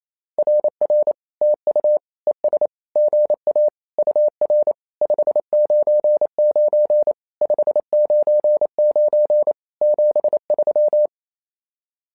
Morse code, heard as a keyed tone, about 28 wpm, 600 Hz. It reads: RR TU ES GA UR 599 599 73
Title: SOTA QSO Practice 28wpm